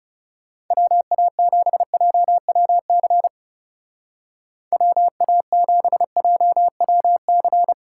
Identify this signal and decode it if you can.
Morse code, heard as a keyed tone, about 38 wpm, 700 Hz. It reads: WA7JWC WA7JWC